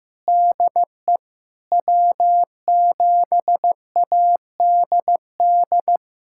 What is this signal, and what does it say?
Morse code, heard as a keyed tone, about 15 wpm, 700 Hz. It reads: DE W7ADD
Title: USA Callsign DE Callsign 15wpm